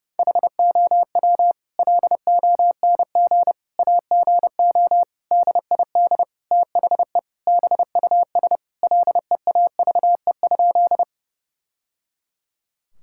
Morse code, heard as a keyed tone, about 30 wpm, 700 Hz. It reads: HOW LONG AGO DID THE BUS LEAVE?
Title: Sentences from Top 1000 Words - Review Part 6 - 30wpm